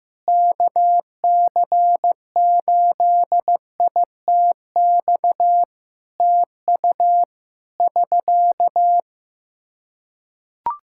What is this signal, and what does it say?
Morse code, heard as a keyed tone, about 15 wpm, 700 Hz. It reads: KC8ITX TU <SK>
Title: Practice QSO - Exchange 4 - Ending - 15wpm